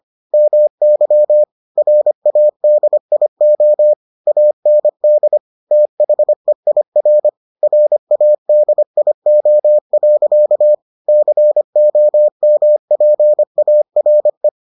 Morse code, heard as a keyed tone, about 25 wpm, 600 Hz. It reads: MY RADIO AND THEIR RADIO. COMPARE